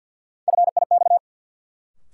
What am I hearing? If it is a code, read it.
Morse code, heard as a keyed tone, about 53 wpm, 700 Hz. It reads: FIX
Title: Sentences from Top 1400 Words - New Word Drill - 50wpm